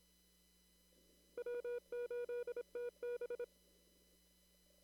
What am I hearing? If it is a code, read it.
Morse code, heard as a keyed tone, about 27 wpm, 480 Hz. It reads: W8TB
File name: CW / kt5j wpx cw-op-k5pi